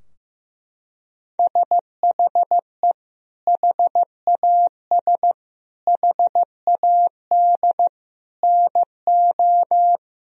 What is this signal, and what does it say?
Morse code, heard as a keyed tone, about 16 wpm, 700 Hz. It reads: SHE HAS HAD NO
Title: Sentences from Top 1000 Words - Review Part 4 - 15wpm